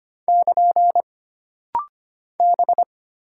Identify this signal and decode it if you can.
Morse code, heard as a keyed tone, about 25 wpm, 700 Hz. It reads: <KN> B
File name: Single Character (Letter:Number:Punctuation:Prosign) - Rapid-Fire - 25wpm